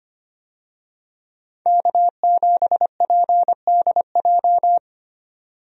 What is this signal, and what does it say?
Morse code, heard as a keyed tone, about 25 wpm, 700 Hz. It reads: K7PDJ
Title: USA Call Signs - 5 Characters - 25wpm